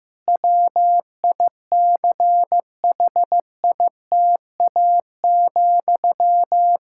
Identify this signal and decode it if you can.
Morse code, heard as a keyed tone, about 16 wpm, 700 Hz. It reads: WICHITA,
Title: K1USN SST QSO Practice 15wpm